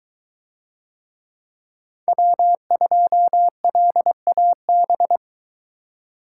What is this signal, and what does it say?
Morse code, heard as a keyed tone, about 22 wpm, 700 Hz. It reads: W2LAB